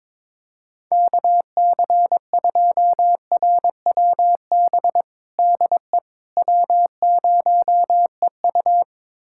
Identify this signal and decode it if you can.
Morse code, heard as a keyed tone, about 22 wpm, 700 Hz. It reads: KC2RWB DE W0EU
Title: USA Callsign DE Callsign 22wpm